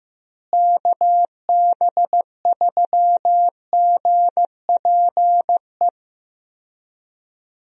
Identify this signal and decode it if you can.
Morse code, heard as a keyed tone, about 15 wpm, 700 Hz. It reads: KB3GPE
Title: USA Call Signs - 6 Characters - Repeated 2x - 15wpm